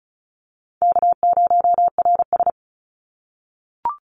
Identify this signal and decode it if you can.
Morse code, heard as a keyed tone, about 34 wpm, 700 Hz. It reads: K0RS
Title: NA SOTA Chaser Callsigns 35wpm